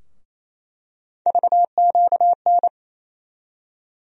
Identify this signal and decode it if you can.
Morse code, heard as a keyed tone, about 28 wpm, 700 Hz. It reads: VQN
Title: USA Call Sign Postfixes - 3 Letters 28wpm